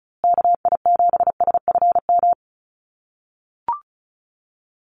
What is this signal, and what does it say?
Morse code, heard as a keyed tone, about 34 wpm, 700 Hz. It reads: KI7SFM
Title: USA Call Signs Repeated 3x 35wpm